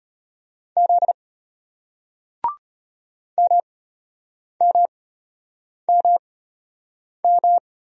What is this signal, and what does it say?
Morse code, heard as a keyed tone, about 34 wpm, spaced as for 8 wpm, 690 Hz. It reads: Z MMMM